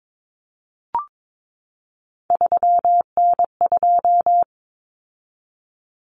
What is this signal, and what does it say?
Morse code, heard as a keyed tone, about 22 wpm, 700 Hz. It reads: 3N2
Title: Common Cut Numbers - 3-Digit 22wpm